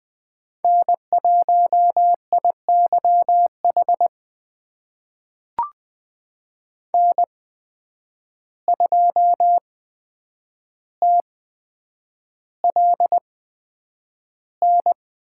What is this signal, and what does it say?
Morse code, heard as a keyed tone, about 20 wpm, spaced as for 9 wpm, 700 Hz. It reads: N1IYH N2TLN
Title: USA Call Signs - 5 Characters - 8x Character Spacing - 20wpm